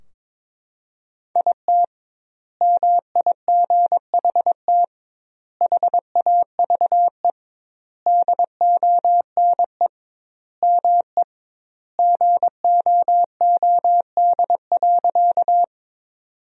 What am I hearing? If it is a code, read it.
Morse code, heard as a keyed tone, about 22 wpm, 700 Hz. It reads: IT MIGHT HAVE DONE ME GOOD.
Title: 2x Word-Spacing - Sentences from Top 500 Words - 22wpm